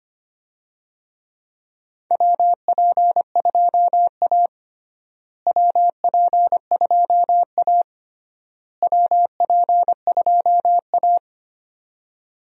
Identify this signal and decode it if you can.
Morse code, heard as a keyed tone, about 25 wpm, 700 Hz. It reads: WP2A WP2A WP2A
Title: USA Call Signs - 4 Characters - Repeated 3x - 25wpm